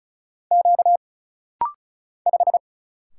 Morse code, heard as a keyed tone, about 34 wpm, 700 Hz. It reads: Q 5
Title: Single Letter-Number Alt - Rapid-Fire 35wpm